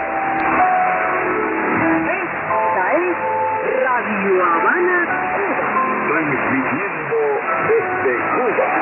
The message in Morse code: none